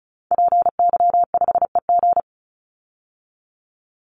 Morse code, heard as a keyed tone, about 34 wpm, 700 Hz. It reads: PY5EG
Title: Call Signs Repeated 2x - 35wpm